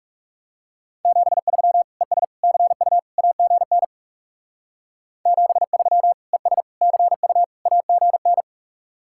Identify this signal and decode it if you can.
Morse code, heard as a keyed tone, about 37 wpm, 700 Hz. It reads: 73 ES CU AGN 73 ES CU AGN
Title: Speed-Racing Common QSO Phrases Encore - 30wpm